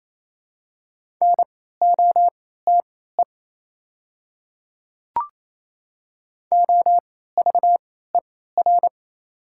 Morse code, heard as a keyed tone, about 28 wpm, spaced as for 16 wpm, 690 Hz. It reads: NOTE OVER
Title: 3x Character Spacing - Top 500 Words - 28wpm